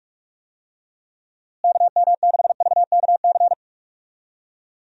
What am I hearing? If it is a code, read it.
Morse code, heard as a keyed tone, about 41 wpm, 700 Hz. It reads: KM6VKC
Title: USA Call Signs - 6 Characters - Repeated 2x - 45wpm